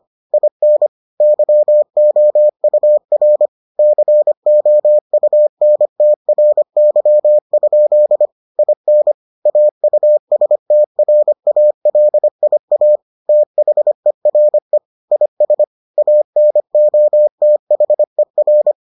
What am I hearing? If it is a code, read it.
Morse code, heard as a keyed tone, about 25 wpm, 600 Hz. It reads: IN YOUR COUNTRY? IN AUSTRALIA THERE IS ANOTHER